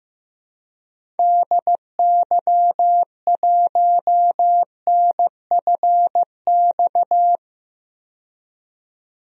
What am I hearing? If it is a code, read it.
Morse code, heard as a keyed tone, about 15 wpm, 700 Hz. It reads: DY1NFX
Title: DXCC-100 Call Signs - 6 Characters - 15wpm